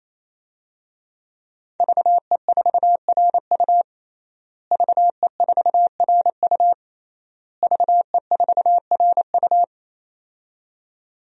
Morse code, heard as a keyed tone, about 28 wpm, 700 Hz. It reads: VE4RU VE4RU VE4RU
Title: DXCC-100 Call Signs - 5 Characters - Repeated 3x - 28wpm